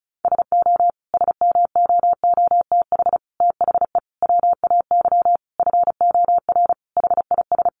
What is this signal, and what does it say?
Morse code, heard as a keyed tone, about 34 wpm, 700 Hz. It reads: SO SMOOTH THE WAY FOR HIS
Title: Sentences from Top 1400 Words 35wpm